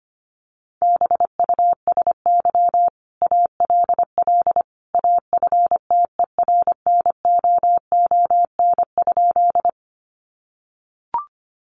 Morse code, heard as a keyed tone, about 25 wpm, 700 Hz. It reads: BUSY ALL AFTERNOON?
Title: Sentences from Top 1500 Words 25wpm